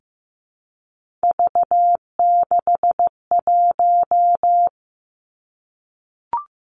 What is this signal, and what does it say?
Morse code, heard as a keyed tone, about 15 wpm, 700 Hz. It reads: V61
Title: Three Challenging Letter-Number Combinations 15wpm